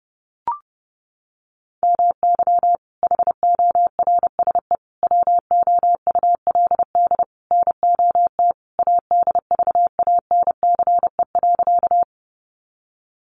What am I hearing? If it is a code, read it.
Morse code, heard as a keyed tone, about 30 wpm, 700 Hz. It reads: MY HORSE WOULD NOT ADVANCE.
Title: Sentences from Top 1400 Words 30wpm